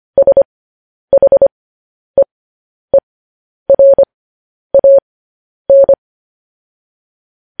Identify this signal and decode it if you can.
Morse code, heard as a keyed tone, about 25 wpm, spaced as for 10 wpm, 550 Hz. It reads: SHEERAN
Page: news headlines in morse code at 10 wpm